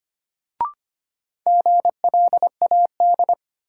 Morse code, heard as a keyed tone, about 25 wpm, 700 Hz. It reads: GLAD
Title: Sentences from Top 800 Words - New Word Drill - 25wpm